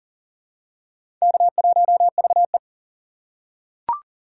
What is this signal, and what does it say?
Morse code, heard as a keyed tone, about 40 wpm, 700 Hz. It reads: K1VE
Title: USA Call Signs - 4 Characters - 3x Character Spacing - 40wpm